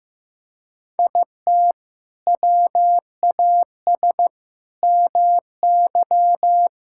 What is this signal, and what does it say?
Morse code, heard as a keyed tone, about 15 wpm, 700 Hz. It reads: IT WAS MY
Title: Sentences from Top 1000 Words - Review Part 5 - 15wpm